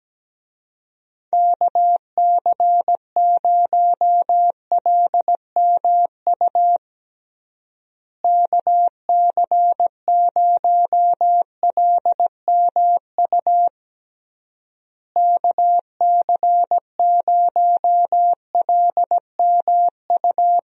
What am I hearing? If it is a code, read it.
Morse code, heard as a keyed tone, about 17 wpm, 700 Hz. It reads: KC0LMU KC0LMU KC0LMU
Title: USA Call Signs - 6 Characters - Repeated 3x - 17wpm